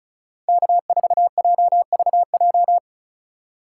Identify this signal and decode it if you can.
Morse code, heard as a keyed tone, about 36 wpm, 700 Hz. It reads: K4JVJ
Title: USA Call Signs - 5 Characters - 3x Character Spacing - 35wpm